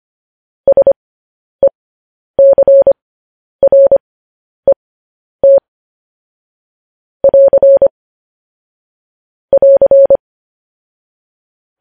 Morse code, heard as a keyed tone, about 25 wpm, spaced as for 10 wpm, 550 Hz. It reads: SECRET <AR> <AR>
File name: podcast / News headlines-1x-25x10